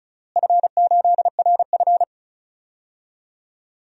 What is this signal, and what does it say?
Morse code, heard as a keyed tone, about 36 wpm, 700 Hz. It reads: F8RF
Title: DXCC-100 Call Signs - 4 Characters - Repeated 3x - 35wpm